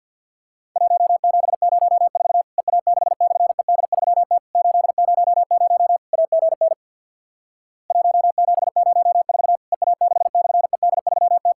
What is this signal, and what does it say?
Morse code, heard as a keyed tone, about 50 wpm, 700 Hz. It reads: 1704 EA6/ED3T 800 AGN 1704 EA6/ED3T
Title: Worked All Europe DX Contest - QTC Exchange with Repeats - 50wpm